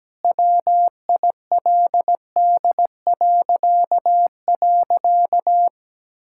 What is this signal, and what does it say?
Morse code, heard as a keyed tone, about 17 wpm, 700 Hz. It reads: WILD..
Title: Pangram Challenge v2 - 17wpm